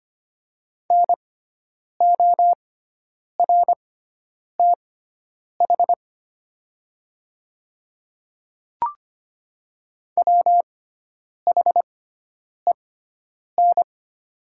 Morse code, heard as a keyed tone, about 25 wpm, spaced as for 9 wpm, 700 Hz. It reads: NORTH WHEN
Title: 6x Character Spacing - Top 300 Words - 25wpm